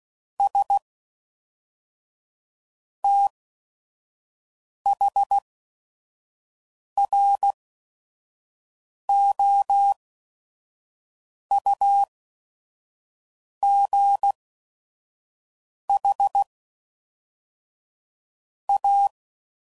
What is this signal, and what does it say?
Morse code, heard as a keyed tone, about 16 wpm, spaced as for 5 wpm, 800 Hz. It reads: S THROUGH A
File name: CW Code Groups @5WPM